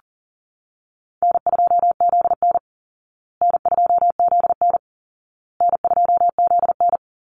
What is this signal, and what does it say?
Morse code, heard as a keyed tone, about 37 wpm, 700 Hz. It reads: N2ZN N2ZN N2ZN
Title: Easy Call Signs Repeated 3x - 40wpm